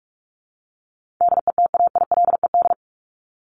Morse code, heard as a keyed tone, about 44 wpm, 700 Hz. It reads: DETAILED
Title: Top 1251-1500 Words 45wpm